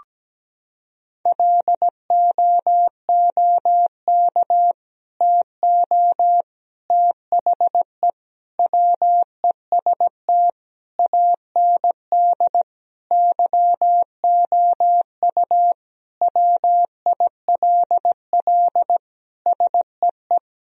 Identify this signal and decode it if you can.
Morse code, heard as a keyed tone, about 17 wpm, 700 Hz. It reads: LOOK TO THE WEST AND YOU WILL SEE
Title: Sentences from Top 1000 Words - Review Part 2 - 17wpm